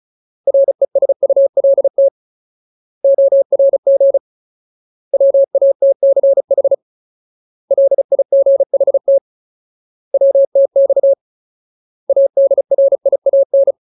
Morse code, heard as a keyed tone, about 34 wpm, 550 Hz. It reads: RESULT ORG WATCH LIGHT WTX ADRIAN
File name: Ultimate Word List / 35WPM Ultimate